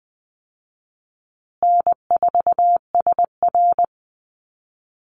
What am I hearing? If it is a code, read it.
Morse code, heard as a keyed tone, about 20 wpm, 700 Hz. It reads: N4SR
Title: Easy Call Signs Repeated 3x - 20wpm